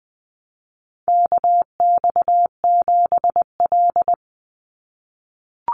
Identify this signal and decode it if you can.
Morse code, heard as a keyed tone, about 20 wpm, 700 Hz. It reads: KX7L
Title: NA SOTA Chaser Callsigns 20wpm